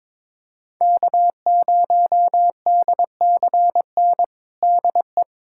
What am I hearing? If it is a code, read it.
Morse code, heard as a keyed tone, about 22 wpm, 700 Hz. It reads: K0DCN DE
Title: USA Callsign DE Callsign - 5 Characters - 22wpm